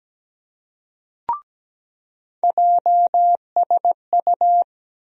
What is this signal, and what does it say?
Morse code, heard as a keyed tone, about 17 wpm, 700 Hz. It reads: JSU
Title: USA Call Sign Postfixes - 3 Letters 17wpm